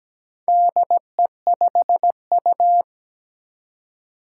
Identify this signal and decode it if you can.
Morse code, heard as a keyed tone, about 17 wpm, 700 Hz. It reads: DE5U
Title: Four Challenging Letter-Number Combinations 17wpm